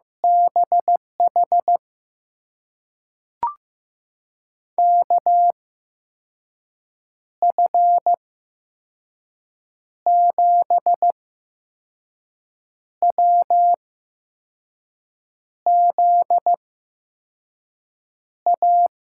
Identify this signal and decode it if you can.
Morse code, heard as a keyed tone, about 15 wpm, spaced as for 5 wpm, 700 Hz. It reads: BH KF7WZA
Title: USA Call Signs - 6 Characters - 8x Character Spacing - 15wpm